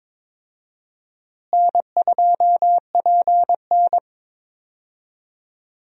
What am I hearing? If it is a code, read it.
Morse code, heard as a keyed tone, about 22 wpm, 700 Hz. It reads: N2PN
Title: USA Call Signs - 4 Characters - Repeated 3x - 22wpm